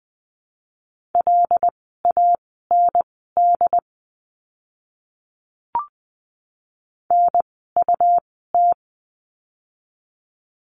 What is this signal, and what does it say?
Morse code, heard as a keyed tone, about 20 wpm, spaced as for 15 wpm, 700 Hz. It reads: LAND NUT
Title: 101 Lesson 17f - Letter u - Words 20wpm